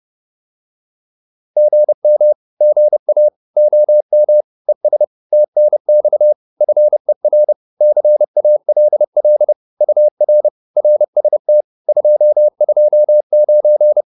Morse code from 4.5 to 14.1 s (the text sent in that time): ES TNX FER CALL UR RST 229